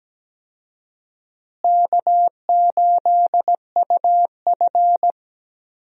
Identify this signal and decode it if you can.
Morse code, heard as a keyed tone, about 17 wpm, 700 Hz. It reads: K8UF
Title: USA Call Signs - 4 Characters - Repeated 2x - 17wpm